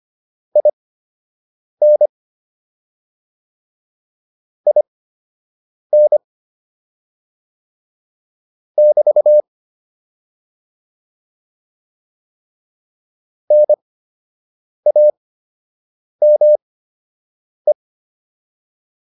Morse code, heard as a keyed tone, about 25 wpm, spaced as for 7 wpm, 600 Hz. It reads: IN IN <BT> NAME